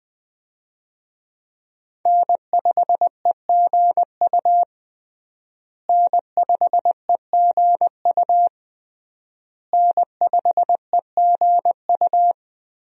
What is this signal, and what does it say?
Morse code, heard as a keyed tone, about 20 wpm, 700 Hz. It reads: N5EGU N5EGU N5EGU
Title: USA Call Signs - 5 Characters - Repeated 3x - 20wpm